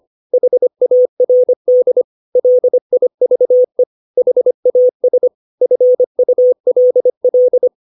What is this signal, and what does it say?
Morse code, heard as a keyed tone, about 25 wpm, 500 Hz. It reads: HARD LIVE HAS FULL
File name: 632 words / 632 Words 25WPM